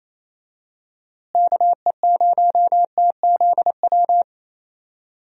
Morse code, heard as a keyed tone, about 27 wpm, 700 Hz. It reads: KE0TZW
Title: USA Call Signs - 6 Characters - Repeated 3x - 28wpm